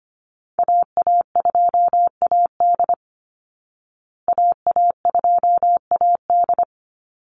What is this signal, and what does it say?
Morse code, heard as a keyed tone, about 25 wpm, 700 Hz. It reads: AA2AD AA2AD
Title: Call Signs Repeated 2x - 25wpm